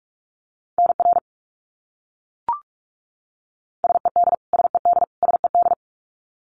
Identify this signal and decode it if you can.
Morse code, heard as a keyed tone, about 44 wpm, 700 Hz. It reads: NR SED SED SED
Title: Abbreviations Repeated 3x - 45wpm